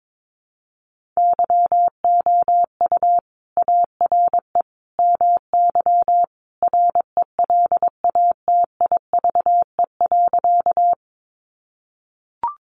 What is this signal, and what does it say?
Morse code, heard as a keyed tone, about 22 wpm, 700 Hz. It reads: YOU ARE MY RELATIVE.
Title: Sentences from Top 1400 Words 22wpm